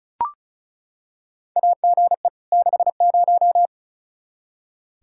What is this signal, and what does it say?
Morse code, heard as a keyed tone, about 34 wpm, 700 Hz. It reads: AGE 60
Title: Practice QSO - Exchange 3 - More Info - 35wpm